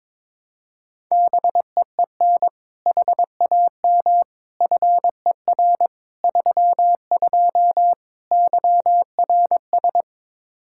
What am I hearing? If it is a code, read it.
Morse code, heard as a keyed tone, about 22 wpm, 700 Hz. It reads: BEEN HAM FER 32 YRS
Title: K1USN SST QSO Practice 22wpm